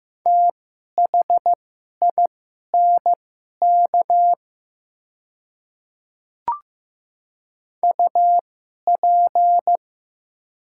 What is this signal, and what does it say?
Morse code, heard as a keyed tone, about 15 wpm, spaced as for 11 wpm, 700 Hz. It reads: THINK UP
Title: 2x Character Spacing - Top 200 Words - 15wpm